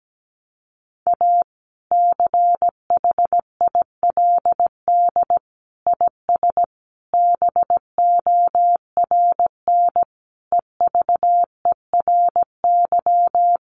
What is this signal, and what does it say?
Morse code, heard as a keyed tone, about 17 wpm, 700 Hz. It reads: A CHILD IS BORN EVERY
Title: Sentences from Top 1400 Words 17wpm